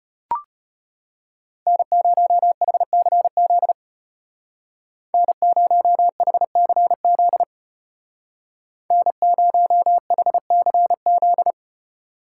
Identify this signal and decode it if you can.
Morse code, heard as a keyed tone, about 34 wpm, 700 Hz. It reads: N0HCZ N0HCZ N0HCZ